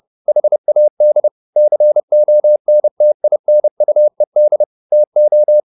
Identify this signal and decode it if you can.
Morse code, heard as a keyed tone, about 30 wpm, 600 Hz. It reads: HAD CONTINUED TO
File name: the old man and the sea / the old man and the sea 1 30wpm